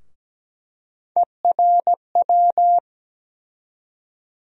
Morse code, heard as a keyed tone, about 17 wpm, 700 Hz. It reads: ERW
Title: USA Call Sign Postfixes - 3 Letters 17wpm